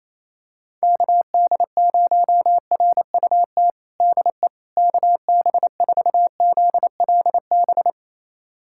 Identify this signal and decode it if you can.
Morse code, heard as a keyed tone, about 27 wpm, 700 Hz. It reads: KD0RUT DE KB4ZLB
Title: USA Callsign DE Callsign - 6 Characters - 28wpm